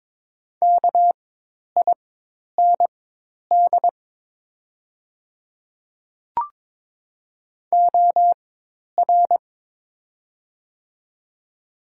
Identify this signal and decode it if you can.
Morse code, heard as a keyed tone, about 22 wpm, spaced as for 10 wpm, 700 Hz. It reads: KIND OR
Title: 4x Character Spacing - Top 200 Words - 22wpm